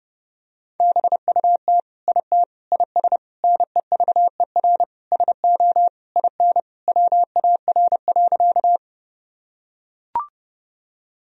Morse code, heard as a keyed tone, about 30 wpm, 700 Hz. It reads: BUT IT IS NEVER SO IN WAR.
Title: Sentences from Top 1000 Words - Review Part 2 - 30wpm